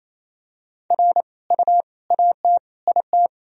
Morse code, heard as a keyed tone, about 27 wpm, 700 Hz. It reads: R U AT IT
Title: Sentences from Top 200 Words and Abbreviations 28wpm